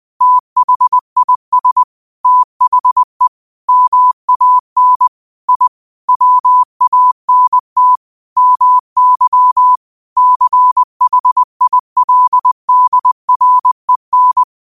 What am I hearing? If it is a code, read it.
Morse code, heard as a keyed tone, about 20 wpm, 1000 Hz. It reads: THIS THE MAN I WANT MY CHILDREN
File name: 20wpm